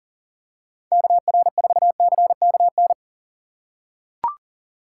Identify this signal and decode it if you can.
Morse code, heard as a keyed tone, about 40 wpm, 700 Hz. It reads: KR4CKN
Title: USA Call Signs - 6 Characters - 2x Character Spacing - 40wpm